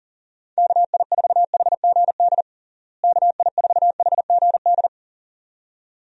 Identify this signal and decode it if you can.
Morse code, heard as a keyed tone, about 37 wpm, 700 Hz. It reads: KI4HGD KI4HGD
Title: USA Call Signs - 6 Characters - Repeated 2x - 40wpm